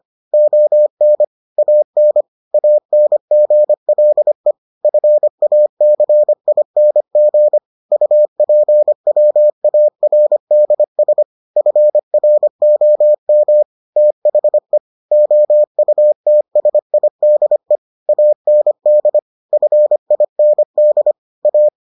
Morse code, heard as a keyed tone, about 25 wpm, 600 Hz. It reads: ON AN ANGLE FACING UPWARDS FROM THE OUTSIDE AND FIND A